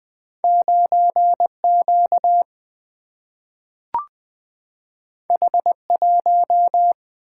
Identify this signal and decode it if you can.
Morse code, heard as a keyed tone, about 20 wpm, 700 Hz. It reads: 9 Q H 1
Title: Single-Letter Single-Number 20wpm